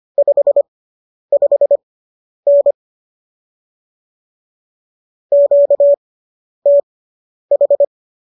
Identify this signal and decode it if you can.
Morse code, heard as a keyed tone, about 25 wpm, spaced as for 10 wpm, 570 Hz. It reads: 55N QTH